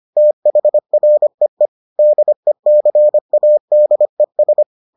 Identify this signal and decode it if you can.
Morse code, heard as a keyed tone, about 25 wpm, 600 Hz. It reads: THREE DECADES